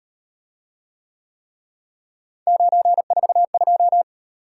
Morse code, heard as a keyed tone, about 38 wpm, 700 Hz. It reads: 942